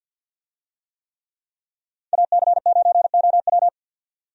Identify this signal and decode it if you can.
Morse code, heard as a keyed tone, about 53 wpm, 700 Hz. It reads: AC9OW
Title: USA Call Signs - 5 Characters - 50wpm